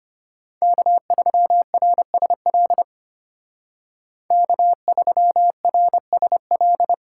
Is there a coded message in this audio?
Yes: Morse code from 0.6 to 6.9 s, heard as a keyed tone, about 32 wpm, 700 Hz. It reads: K3RSL K3RSL